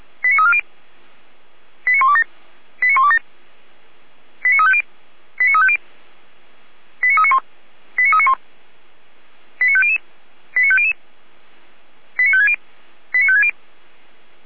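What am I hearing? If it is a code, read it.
Five-tone selective call (ZVEI), heard as quick runs of five tones, tones of about 70 ms. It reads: 89340 89127 89127 89350 89350 89391 89391 89500 89500 89560 89560
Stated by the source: ZVEI five-tone selective call received on an SDR.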